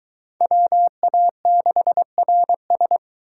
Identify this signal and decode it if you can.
Morse code, heard as a keyed tone, about 22 wpm, 700 Hz. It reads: WA6RS